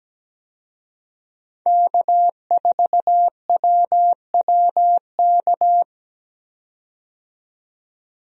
Morse code, heard as a keyed tone, about 17 wpm, 700 Hz. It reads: K4WWK
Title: USA Call Signs - 5 Characters - 17wpm